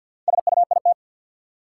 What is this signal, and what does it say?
Morse code, heard as a keyed tone, about 51 wpm, 700 Hz. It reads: SUIT
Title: Sentences from Top 1600 Words - New Word Drill - 50wpm